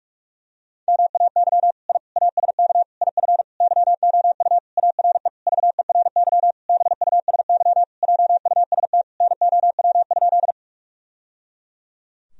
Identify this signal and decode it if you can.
Morse code, heard as a keyed tone, about 41 wpm, 700 Hz. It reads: MAY I ASK IF YOU ARE VERY BUSY JUST NOW?
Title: Sentences from Top 1000 Words - Review Part 3 - 45wpm